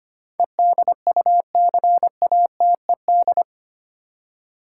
Morse code, heard as a keyed tone, about 25 wpm, 700 Hz. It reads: EDUCATED
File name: 8 Letter Words - First 3 Letters Given - 25wpm